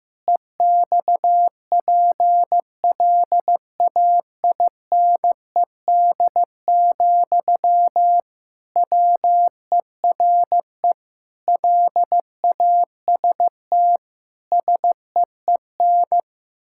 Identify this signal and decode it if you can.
Morse code, heard as a keyed tone, about 14 wpm, 700 Hz. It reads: EXPLAINED, WERE LAST SEEN
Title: Pangram Challenge v2 - 15wpm